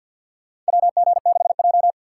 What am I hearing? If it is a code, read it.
Morse code, heard as a keyed tone, about 50 wpm, 700 Hz. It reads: WG6J
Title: USA Call Signs - 6x Character Spacing - 50wpm